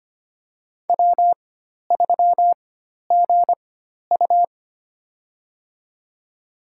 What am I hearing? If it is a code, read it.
Morse code, heard as a keyed tone, about 25 wpm, spaced as for 12 wpm, 700 Hz. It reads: W3GU
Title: USA Call Signs - 4 Characters - 4x Character Spacing - 25wpm